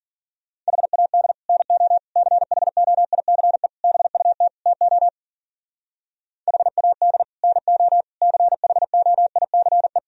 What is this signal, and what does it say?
Morse code, heard as a keyed tone, about 50 wpm, 700 Hz. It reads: HAD NO CHOICE BUT TO HAD NO CHOICE
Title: Speed-Racing - Sets of 5 Words 40wpm